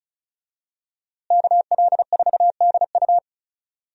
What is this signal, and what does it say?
Morse code, heard as a keyed tone, about 36 wpm, 700 Hz. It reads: KL4DU
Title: USA Call Signs - 5 Characters - Repeated 3x - 35wpm